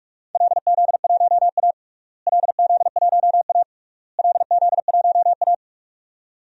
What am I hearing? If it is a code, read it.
Morse code, heard as a keyed tone, about 46 wpm, 700 Hz. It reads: LZ1A LZ1A LZ1A
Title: DXCC-100 Call Signs - 4 Characters - Repeated 3x - 45wpm